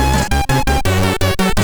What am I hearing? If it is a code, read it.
There is no code.